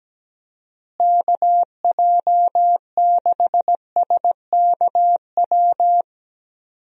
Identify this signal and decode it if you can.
Morse code, heard as a keyed tone, about 17 wpm, 700 Hz. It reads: KJ6SKW
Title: USA Call Signs - 6 Characters - Repeated 3x - 17wpm